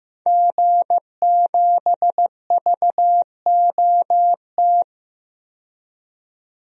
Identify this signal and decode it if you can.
Morse code, heard as a keyed tone, about 15 wpm, 700 Hz. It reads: G7VOT
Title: DXCC-100 Call Signs - 5 Characters - 15wpm